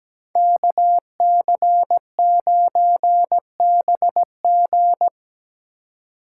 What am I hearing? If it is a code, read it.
Morse code, heard as a keyed tone, about 17 wpm, 700 Hz. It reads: KC9BG
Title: K1USN SST QSO Practice - 17wpm